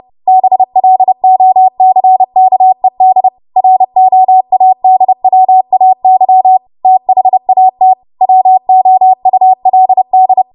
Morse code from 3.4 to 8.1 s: ROADWAY THAT